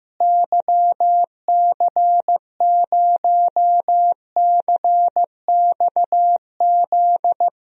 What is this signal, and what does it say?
Morse code, heard as a keyed tone, about 15 wpm, 700 Hz. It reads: YC0CXZ
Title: DXCC-100 Call Signs - 6 Characters - Repeated 2x - 15wpm